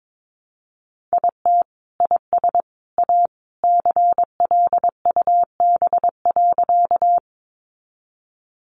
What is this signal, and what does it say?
Morse code, heard as a keyed tone, about 22 wpm, 700 Hz. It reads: IT IS A CLUB.
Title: Sentences from Top 1200 Words 22wpm